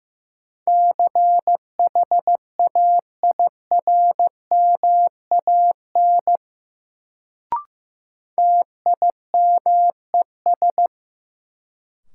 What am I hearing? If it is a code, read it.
Morse code, heard as a keyed tone, about 15 wpm, 700 Hz. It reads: CHAIRMAN TIMES
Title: Sets of 3 Words - New Word Drill - 15wpm